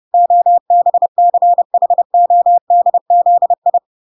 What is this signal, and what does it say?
Morse code, heard as a keyed tone, about 30 wpm, 700 Hz. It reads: OBCHODZI